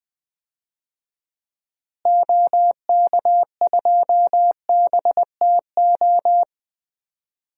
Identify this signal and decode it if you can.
Morse code, heard as a keyed tone, about 20 wpm, 700 Hz. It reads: OK2BTO
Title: DXCC-100 Call Signs - 6 Characters - Repeated 2x - 20wpm